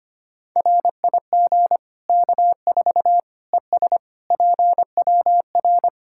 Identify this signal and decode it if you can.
Morse code, heard as a keyed tone, about 25 wpm, 700 Hz. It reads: RIG K4 ES PWR
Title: Practice QSO - Exchange 2 - Info - 25wpm